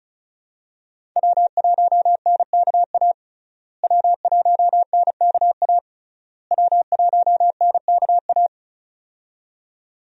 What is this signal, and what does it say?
Morse code, heard as a keyed tone, about 34 wpm, 700 Hz. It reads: W1NKA W1NKA W1NKA
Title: USA Call Signs - 5 Characters - Repeated 3x - 35wpm